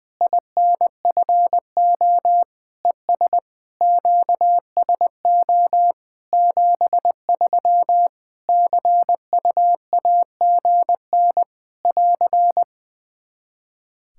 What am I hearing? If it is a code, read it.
Morse code, heard as a keyed tone, about 20 wpm, 700 Hz. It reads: INFO ES QSO 73 CUAGN <AR>
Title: Practice QSO - Exchange 4 - Ending - 20wpm